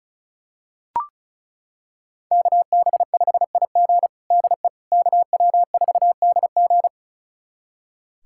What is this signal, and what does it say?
Morse code, heard as a keyed tone, about 34 wpm, 700 Hz. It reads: KB5IG DE KW4DG
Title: USA Callsign DE Callsign - 5 Characters - 35wpm